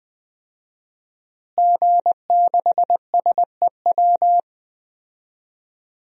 Morse code, heard as a keyed tone, about 20 wpm, 700 Hz. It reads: G6SEW
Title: DXCC-100 Call Signs - 5 Characters - Repeated 2x - 20wpm